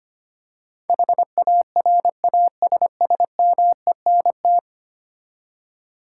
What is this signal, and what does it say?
Morse code, heard as a keyed tone, about 25 wpm, 700 Hz. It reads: HARASSMENT
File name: Ten Letter Words - 25wpm